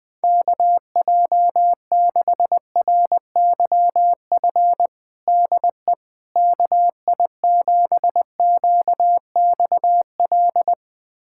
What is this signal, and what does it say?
Morse code, heard as a keyed tone, about 20 wpm, 700 Hz. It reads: KJ6RYF DE KI7QXL
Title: USA Callsign DE Callsign - 6 Characters - 20wpm